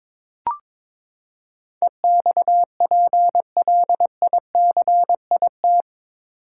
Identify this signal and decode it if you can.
Morse code, heard as a keyed tone, about 22 wpm, 700 Hz. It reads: EXPLICIT
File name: practice / 22wpm / Eight Letter Words - 22wpm